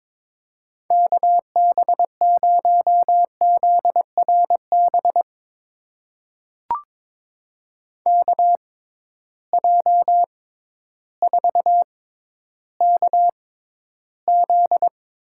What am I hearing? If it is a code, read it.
Morse code, heard as a keyed tone, about 22 wpm, spaced as for 12 wpm, 700 Hz. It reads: KB0ZRB KJ4KZ
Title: USA Call Signs - 6 Characters - 6x Character Spacing - 22wpm